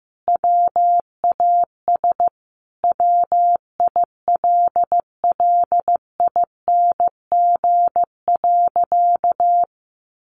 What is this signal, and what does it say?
Morse code, heard as a keyed tone, about 15 wpm, 700 Hz. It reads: WAS WILLING.
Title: Sentences from Top 1400 Words 15wpm